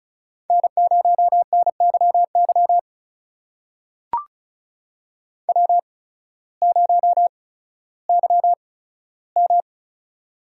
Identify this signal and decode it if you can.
Morse code, heard as a keyed tone, about 36 wpm, spaced as for 16 wpm, 700 Hz. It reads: N0NYY W0YM
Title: USA Call Signs - 8x Character Spacing - 35wpm